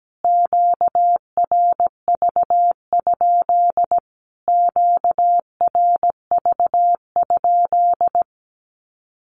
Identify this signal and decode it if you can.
Morse code, heard as a keyed tone, about 17 wpm, 700 Hz. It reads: QRV? QRV?
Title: Q-Codes Repeated 3x 17wpm